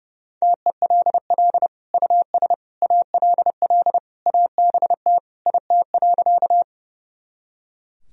Morse code, heard as a keyed tone, about 30 wpm, 700 Hz. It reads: TELL US ALL ABT IT.
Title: Sentences from Top 200 Words and Abbreviations 30wpm